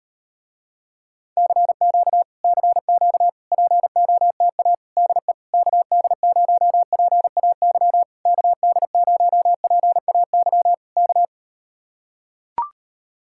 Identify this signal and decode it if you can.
Morse code, heard as a keyed tone, about 38 wpm, 700 Hz. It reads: CQ CQ POTA DE KD0PAY KD0PAY K